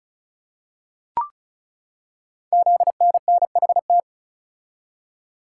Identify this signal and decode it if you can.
Morse code, heard as a keyed tone, about 36 wpm, 700 Hz. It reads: ZNNHT
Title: Code Groups - 5 Letters - 35wpm